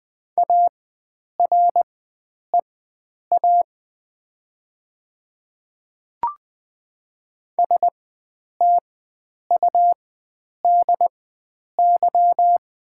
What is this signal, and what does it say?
Morse code, heard as a keyed tone, about 20 wpm, spaced as for 9 wpm, 700 Hz. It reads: AREA STUDY
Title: 4x Character Spacing - Top 500 Words - 20wpm